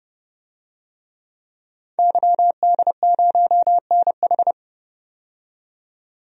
Morse code, heard as a keyed tone, about 30 wpm, 700 Hz. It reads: YD0NH
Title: DXCC-100 Call Signs - 5 Characters - 30wpm